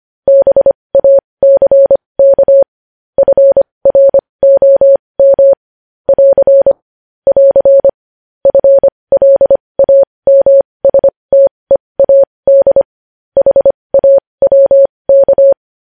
Morse code, heard as a keyed tone, about 25 wpm, spaced as for 20 wpm, 550 Hz. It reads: BACK FROM <AR> <AR> FLAMSTEAD HAWK
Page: News headlines in morse code at 20 wpm